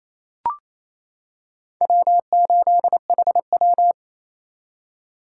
Morse code, heard as a keyed tone, about 28 wpm, 700 Hz. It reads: W8HW
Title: K1USN SST QSO Practice - 28wpm